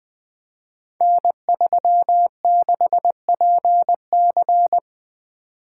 Morse code, heard as a keyed tone, about 20 wpm, 700 Hz. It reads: N36PC
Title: Code Groups - 5 Letter:Numbers - 20wpm